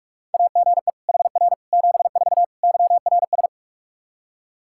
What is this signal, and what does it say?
Morse code, heard as a keyed tone, about 46 wpm, 700 Hz. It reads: AGE HR 74 YRS
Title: Practice QSO - Exchange 3 - More Info - 45wpm